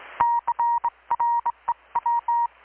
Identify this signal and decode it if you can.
Morse code, heard as a keyed tone, about 21 wpm, 940 Hz. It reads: CREW
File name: blog / swl / 6-abril-2019-12695-USB-KPS